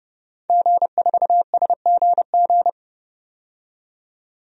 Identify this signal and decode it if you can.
Morse code, heard as a keyed tone, about 30 wpm, 700 Hz. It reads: G4SGG
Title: DXCC-100 Call Signs - 5 Characters - 30wpm